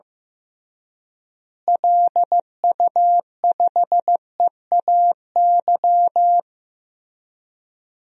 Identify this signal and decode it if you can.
Morse code, heard as a keyed tone, about 15 wpm, 700 Hz. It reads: LU5EAY
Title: DXCC-100 Call Signs - 6 Characters - Repeated 3x - 15wpm